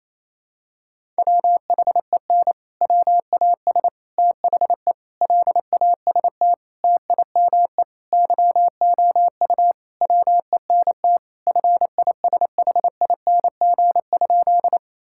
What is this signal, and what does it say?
Morse code, heard as a keyed tone, about 28 wpm, 700 Hz. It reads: WHEN WAS THE LAST TIME YOU WENT FISHING?
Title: Sentences from Top 1000 Words - Review Part 5 - 28wpm